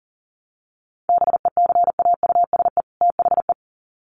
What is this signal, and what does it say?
Morse code, heard as a keyed tone, about 40 wpm, 700 Hz. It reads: BECAUSE THE
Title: Sets of 2 Words 40wpm